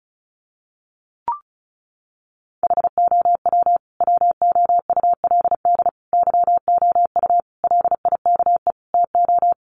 Morse code, heard as a keyed tone, about 34 wpm, 700 Hz. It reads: HOW WOULD YOU LIKE TO
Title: Sentences from Top 100 Words 35wpm